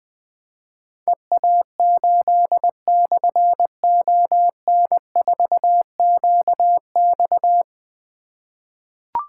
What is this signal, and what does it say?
Morse code, heard as a keyed tone, about 21 wpm, 700 Hz. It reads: EA8/ON4QX
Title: Difficult Call Signs 20wpm